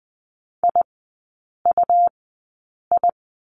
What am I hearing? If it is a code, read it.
Morse code, heard as a keyed tone, about 20 wpm, spaced as for 8 wpm, 700 Hz. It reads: IUI
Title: 100 Lesson 17e - Letter u - Three Characters 20wpm